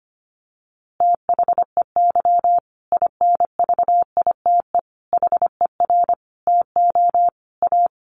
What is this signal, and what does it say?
Morse code, heard as a keyed tone, about 25 wpm, 700 Hz. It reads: THEY INVITE HER TO A
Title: Easy Phrases 25wpm